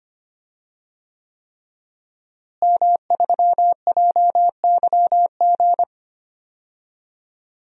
Morse code, heard as a keyed tone, about 25 wpm, 700 Hz. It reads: M3JYG
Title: DXCC-100 Call Signs - 5 Characters - Repeated 3x - 25wpm